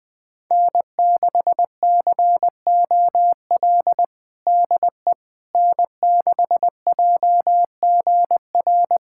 Morse code, heard as a keyed tone, about 20 wpm, 700 Hz. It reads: N6COL DE N6JGR
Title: USA Callsign DE Callsign - 5 Characters - 20wpm